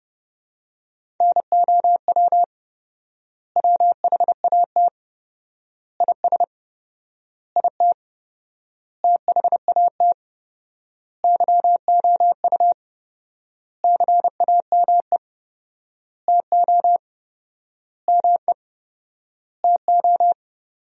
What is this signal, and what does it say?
Morse code, heard as a keyed tone, about 30 wpm, 700 Hz. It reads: NOW WHAT IS IT THAT YOU CAME TO ME TO
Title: 4x Word-Spacing - Sentences from Top 200 Words - 30wpm